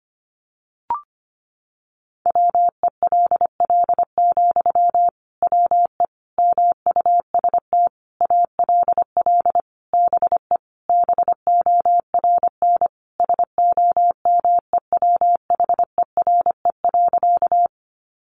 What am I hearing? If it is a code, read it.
Morse code, heard as a keyed tone, about 25 wpm, 700 Hz. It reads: WELL, WE MUST ALL BE BORN SOMEWHERE.
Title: Sentences from Top 1400 Words 25wpm